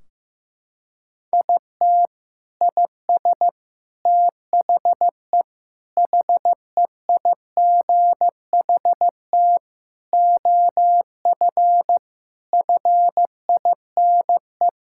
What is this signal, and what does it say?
Morse code, heard as a keyed tone, about 15 wpm, 700 Hz. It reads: IT IS THE HEIGHT OF FINE
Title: Sentences from Top 1600 Words 15wpm